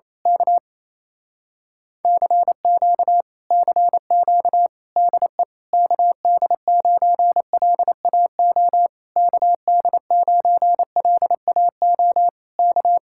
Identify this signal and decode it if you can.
Morse code, heard as a keyed tone, about 27 wpm, 700 Hz. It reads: K CQ CQ DE KD9LAO KD9LAO K
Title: Speed-Racing Common QSO Phrases Encore - 28wpm